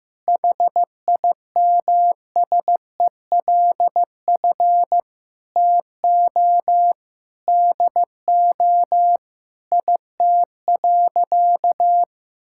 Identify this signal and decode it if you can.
Morse code, heard as a keyed tone, about 15 wpm, 700 Hz. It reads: HIMSELF TO DO IT.
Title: Sentences from Top 1600 Words 15wpm